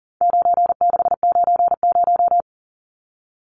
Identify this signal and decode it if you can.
Morse code, heard as a keyed tone, about 40 wpm, 700 Hz. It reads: 9690
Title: Numbers - 4-Digit 40wpm